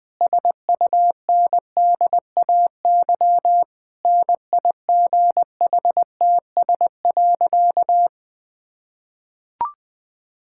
Morse code, heard as a keyed tone, about 19 wpm, 700 Hz. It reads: SUNDAY NIGHTS.
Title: Difficult Sentences with Hints - 20wpm